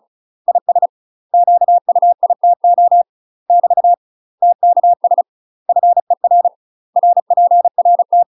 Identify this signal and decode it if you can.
Morse code, heard as a keyed tone, about 36 wpm, 700 Hz. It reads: IS QUITO <BT> TKS FER RPRT
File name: UK MASTER QSOs TRAINING / 35WPM / UK MASTER QSOs 0007